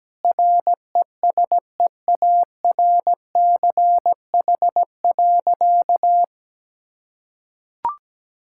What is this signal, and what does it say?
Morse code, heard as a keyed tone, about 17 wpm, 700 Hz. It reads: RESEARCH.
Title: Sentences from Top 1000 Words - Review Part 5 - 17wpm